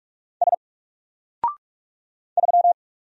Morse code, heard as a keyed tone, about 44 wpm, 700 Hz. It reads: S 3
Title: Single Letter-Number Alt 45wpm